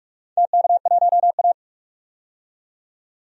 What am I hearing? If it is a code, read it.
Morse code, heard as a keyed tone, about 41 wpm, 700 Hz. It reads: TK1A
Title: DXCC-100 Call Signs - 4 Characters - 45wpm